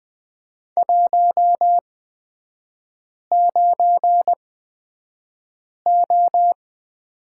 Morse code, heard as a keyed tone, about 20 wpm, spaced as for 5 wpm, 700 Hz. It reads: 19O
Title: 236 Lesson 38e - Letter q - Three Characters 20wpm